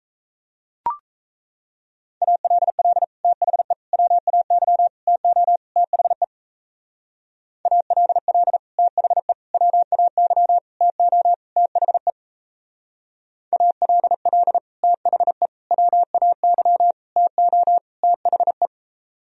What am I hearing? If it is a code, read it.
Morse code, heard as a keyed tone, about 34 wpm, 700 Hz. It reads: ALL THE WAY TO THE ALL THE WAY TO THE ALL THE WAY TO THE
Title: Speed-Racing - Sets of 5 Words 28wpm